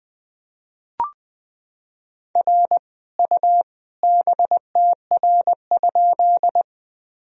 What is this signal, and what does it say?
Morse code, heard as a keyed tone, about 21 wpm, 700 Hz. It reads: R U BTR?
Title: Sentences from Top 400 Words and Abbreviations 20wpm